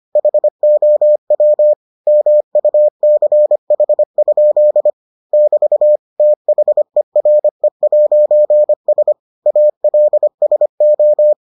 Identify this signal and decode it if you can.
Morse code, heard as a keyed tone, about 25 wpm, 600 Hz. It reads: HOW MUCH? <BT> THERE'S ALSO